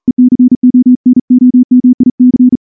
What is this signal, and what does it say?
Morse code, heard as a keyed tone, about 38 wpm, 250 Hz. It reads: ECONOMIC
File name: morse-training-40WPM